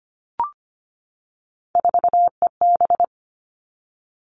Morse code, heard as a keyed tone, about 25 wpm, 700 Hz. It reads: 4EB
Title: All Cut Numbers - 3-Digit 25wpm